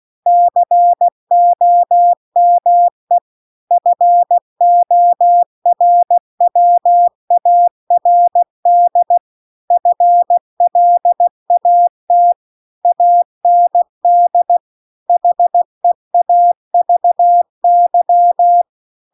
Morse code, heard as a keyed tone, about 16 wpm, 700 Hz. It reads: COME FORWARD FLAT AND HEAVY
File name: CW-The-Big-Twoheaded-River-2-16WPM